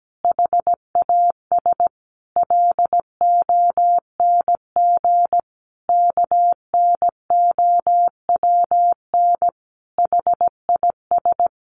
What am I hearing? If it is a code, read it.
Morse code, heard as a keyed tone, about 17 wpm, 700 Hz. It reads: HAS LONG KNOWN HIS
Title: Sentences from Top 1200 Words 17wpm